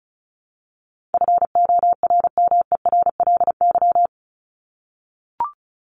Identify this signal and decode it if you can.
Morse code, heard as a keyed tone, about 34 wpm, 700 Hz. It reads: FORMERLY
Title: LY Suffix - Top 100 Words with Base Words 35wpm